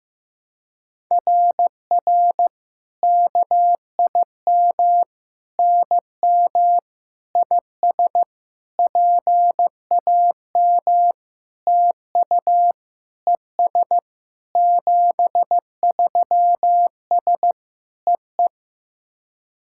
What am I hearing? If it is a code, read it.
Morse code, heard as a keyed tone, about 16 wpm, 700 Hz. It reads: RR KIM NM IS PAM TU ES 73S EE
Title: SOTA QSO Practice 15wpm